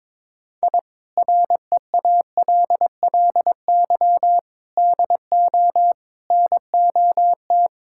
Morse code, heard as a keyed tone, about 22 wpm, 700 Hz. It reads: I REALLY DO NOT
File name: practice / 22wpm / Sentences with 4-Word Phrases - 22wpm